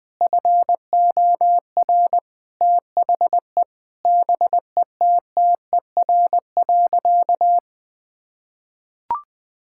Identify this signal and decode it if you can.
Morse code, heard as a keyed tone, about 20 wpm, 700 Hz. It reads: FOR THE BETTER.
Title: Difficult Sentences with Hints - 20wpm